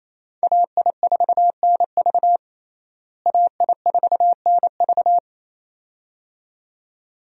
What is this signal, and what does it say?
Morse code, heard as a keyed tone, about 28 wpm, 700 Hz. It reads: AI4NV AI4NV
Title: USA Call Signs - 5 Characters - Repeated 2x - 28wpm